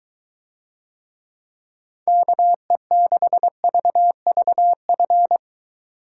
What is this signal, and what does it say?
Morse code, heard as a keyed tone, about 23 wpm, 700 Hz. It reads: KE6VVF